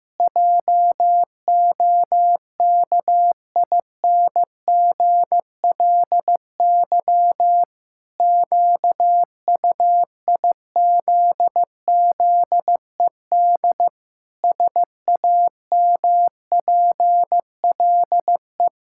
Morse code, heard as a keyed tone, about 15 wpm, 700 Hz. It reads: JOKINGLY QUIZZED SAMPLE
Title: Pangram Challenge v2 - 15wpm